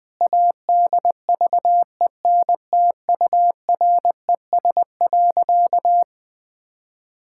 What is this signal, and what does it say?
Morse code, heard as a keyed tone, about 20 wpm, 700 Hz. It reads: ADVENTURES.
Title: Sentences with 2-Word Phrases - 20wpm